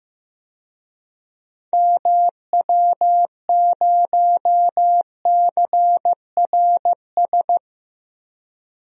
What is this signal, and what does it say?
Morse code, heard as a keyed tone, about 15 wpm, 700 Hz. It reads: MW0CRS
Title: DXCC-100 Call Signs - 6 Characters - Repeated 2x - 15wpm